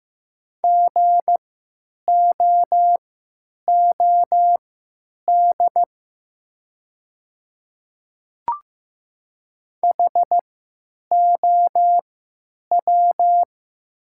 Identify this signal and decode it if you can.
Morse code, heard as a keyed tone, about 15 wpm, spaced as for 9 wpm, 700 Hz. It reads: GOOD HOW
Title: 3x Character Spacing - Top 500 Words - 15wpm